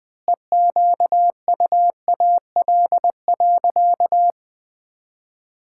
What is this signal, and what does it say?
Morse code, heard as a keyed tone, about 20 wpm, 700 Hz. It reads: EQUAL.
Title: Sentences with 9-Word Phrases - Spoken Beginning - 20wpm